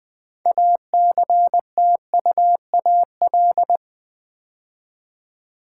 Morse code, heard as a keyed tone, about 20 wpm, 700 Hz. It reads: ACTUAL
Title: Build A Word - 2 Letters At A Time - 8-Letter Words - 20wpm